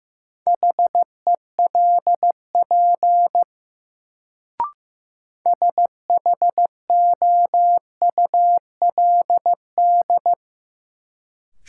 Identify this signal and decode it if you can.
Morse code, heard as a keyed tone, about 16 wpm, 700 Hz. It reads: HELP SHOULD
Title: Sets of 2 Words - New Word Drill - 15wpm